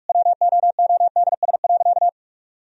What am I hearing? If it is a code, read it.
Morse code, heard as a keyed tone, about 46 wpm, 700 Hz. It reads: WOODS.
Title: Sentences from Top 1000 Words - Review Part 2 - 45wpm